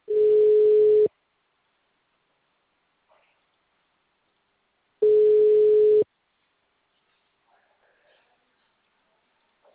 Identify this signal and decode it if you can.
Ringback tone (Europe, 425 Hz).
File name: out- -2008- - - .0